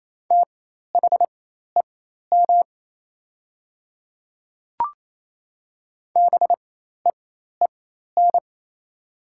Morse code, heard as a keyed tone, about 28 wpm, spaced as for 13 wpm, 690 Hz. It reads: THEM BEEN
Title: 4x Character Spacing - Top 100 Words - 28wpm